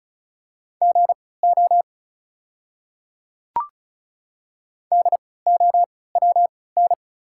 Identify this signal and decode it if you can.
Morse code, heard as a keyed tone, about 36 wpm, spaced as for 20 wpm, 690 Hz. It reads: GO DOWN
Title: 3x Character Spacing - Top 100 Words - 35wpm